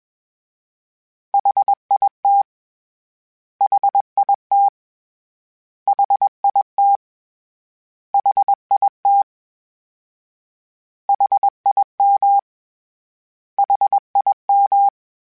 Morse code, heard as a keyed tone, about 21 wpm, 800 Hz. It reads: HIT HIT HIT HIT HIM HIM
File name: Short Runs / Short Runs-mp3s / LongContrasts @20